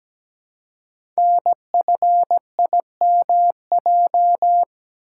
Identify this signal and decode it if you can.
Morse code, heard as a keyed tone, about 17 wpm, 700 Hz. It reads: NFIMJ
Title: Code Groups - 5 Letters - 17wpm